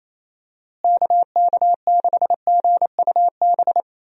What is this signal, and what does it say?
Morse code, heard as a keyed tone, about 28 wpm, 700 Hz. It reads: KK6GUB
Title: USA Call Signs - 6 Characters - Repeated 3x - 28wpm